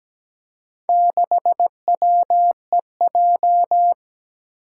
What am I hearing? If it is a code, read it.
Morse code, heard as a keyed tone, about 17 wpm, 700 Hz. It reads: 6WEJ
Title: Four Challenging Letter-Number Combinations 17wpm